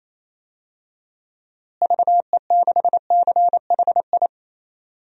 Morse code, heard as a keyed tone, about 28 wpm, 700 Hz. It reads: VE6CHI
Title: DXCC-100 Call Signs - 6 Characters - Repeated 2x - 28wpm